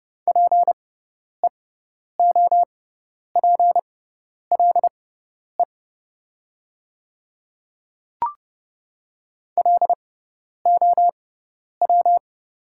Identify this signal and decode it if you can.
Morse code, heard as a keyed tone, about 30 wpm, spaced as for 10 wpm, 700 Hz. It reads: PEOPLE LOW
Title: 6x Character Spacing - Top 200 Words - 30wpm